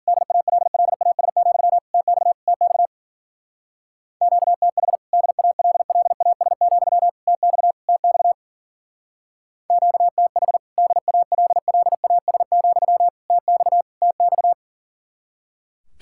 Morse code, heard as a keyed tone, about 49 wpm, 700 Hz. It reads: DALLAS, TX TX QTH DALLAS, TX TX QTH DALLAS, TX TX
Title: Speed-Racing Common QSO Phrases - 40wpm